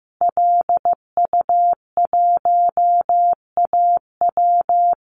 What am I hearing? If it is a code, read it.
Morse code, heard as a keyed tone, about 15 wpm, 700 Hz. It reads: LU1AW
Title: Call Signs Repeated 2x - 15wpm